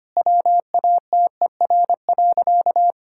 Morse code, heard as a keyed tone, about 25 wpm, 690 Hz. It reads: WATER.
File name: Easy Sentences with Hints - 25wpm